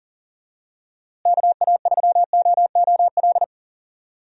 Morse code, heard as a keyed tone, about 38 wpm, 700 Hz. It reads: KA3OOL
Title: USA Call Signs - 6 Characters - Repeated 3x - 40wpm